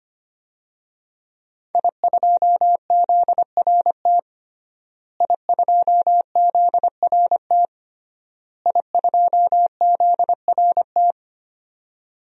Morse code, heard as a keyed tone, about 25 wpm, 700 Hz. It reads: I2ZRT I2ZRT I2ZRT
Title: DXCC-100 Call Signs - 5 Characters - Repeated 3x - 25wpm